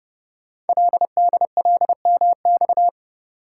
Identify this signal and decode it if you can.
Morse code, heard as a keyed tone, about 30 wpm, 700 Hz. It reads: LDLMX
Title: Code Groups - 5 Letters - 30wpm